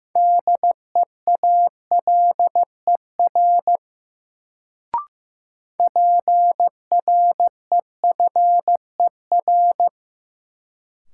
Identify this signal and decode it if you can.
Morse code, heard as a keyed tone, about 15 wpm, 700 Hz. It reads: DEALER PREFER
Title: Sentences from Top 1100 Words - New Word Drill Alt - 15wpm